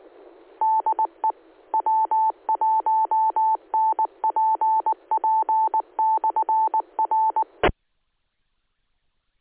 Morse code, heard as a keyed tone, about 19 wpm, 880 Hz. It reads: DE W1NPP/R
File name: ares / AARES / aaresnet-feb-18.2016